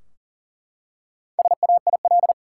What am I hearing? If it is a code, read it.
Morse code, heard as a keyed tone, about 40 wpm, 700 Hz. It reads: SAIL
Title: Four Letter Words - 40wpm